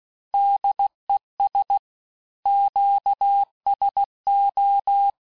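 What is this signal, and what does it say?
Morse code, heard as a keyed tone, about 16 wpm, 800 Hz. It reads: DES QSO
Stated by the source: Morse course text, end of QSO, sent at 15 words per minute.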